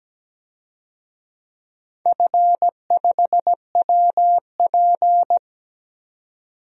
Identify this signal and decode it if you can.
Morse code, heard as a keyed tone, about 17 wpm, 700 Hz. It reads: F5WP
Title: DXCC-100 Call Signs - 4 Characters - 17wpm